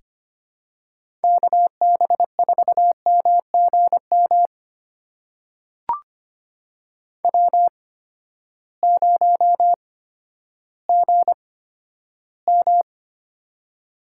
Morse code, heard as a keyed tone, about 25 wpm, spaced as for 12 wpm, 700 Hz. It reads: KB4MGM W0GM
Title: USA Call Signs - 8x Character Spacing - 25wpm